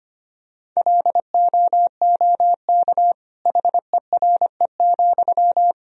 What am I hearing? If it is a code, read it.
Morse code, heard as a keyed tone, about 25 wpm, 700 Hz. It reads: LOOK HERE,
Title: Sentences from Top 1000 Words - Review Part 1 - 25wpm